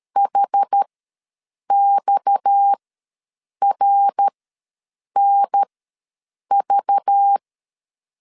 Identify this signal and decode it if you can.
Morse code, heard as a keyed tone, about 13 wpm, spaced as for 7 wpm, 800 Hz. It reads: HXRNV
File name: LEZIONI / 6 PKX0 / 6 PKX ws 12wpm 3106